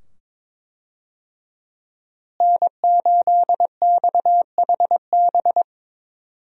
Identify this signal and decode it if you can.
Morse code, heard as a keyed tone, about 22 wpm, 700 Hz. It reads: N8XHB
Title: USA Call Signs - 5 Characters - Repeated 2x - 22wpm